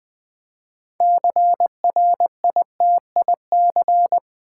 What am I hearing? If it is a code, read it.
Morse code, heard as a keyed tone, about 20 wpm, 700 Hz. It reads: CRITIC
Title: Build A Word - 2 Letters At A Time - 8-Letter Words - 20wpm